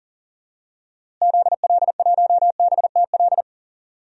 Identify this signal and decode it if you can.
Morse code, heard as a keyed tone, about 40 wpm, 700 Hz. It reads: ZL1BTL
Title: DXCC-100 Call Signs - 6 Characters - Repeated 3x - 40wpm